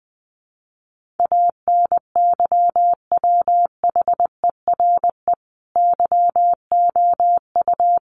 Morse code, heard as a keyed tone, about 20 wpm, 700 Hz. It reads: ANYWHERE YOU
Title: Sentences from Top 1400 Words 20wpm